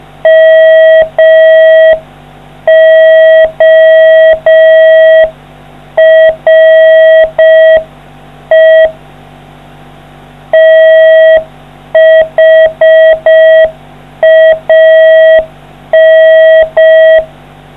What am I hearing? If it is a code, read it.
Morse code, heard as a keyed tone, about 5 wpm, 640 Hz. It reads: MORE THAN